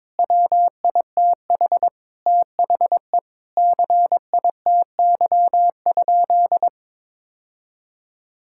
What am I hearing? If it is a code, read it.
Morse code, heard as a keyed tone, about 22 wpm, 700 Hz. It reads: WITH THE CITY?
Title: Sentences from Top 1600 Words 22wpm